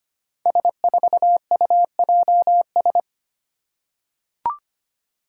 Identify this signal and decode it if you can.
Morse code, heard as a keyed tone, about 25 wpm, 700 Hz. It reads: S4UJS
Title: Five Challenging Letter-Number Combinations 25wpm